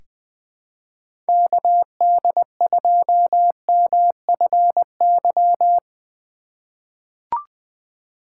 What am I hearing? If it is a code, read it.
Morse code, heard as a keyed tone, about 20 wpm, 700 Hz. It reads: KD2MFY
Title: USA Call Signs - 6 Characters - 2x Character Spacing - 20wpm